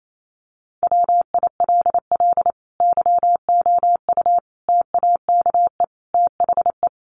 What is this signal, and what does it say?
Morse code, heard as a keyed tone, about 28 wpm, 700 Hz. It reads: WILL YOU TAKE THE
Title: Sentences from Top 1300 Words 28wpm